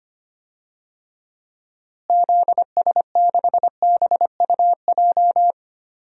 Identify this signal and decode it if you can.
Morse code, heard as a keyed tone, about 25 wpm, 700 Hz. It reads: ZS6BUJ
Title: DXCC-100 Call Signs - 6 Characters - Repeated 3x - 25wpm